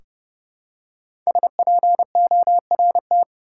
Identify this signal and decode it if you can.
Morse code, heard as a keyed tone, about 31 wpm, 700 Hz. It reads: SPORT
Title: Five Letter Words - 30wpm